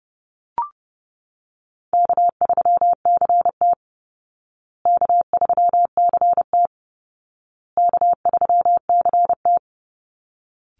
Morse code, heard as a keyed tone, about 30 wpm, 700 Hz. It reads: K3CT K3CT K3CT
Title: Call Signs Repeated 2x - 30wpm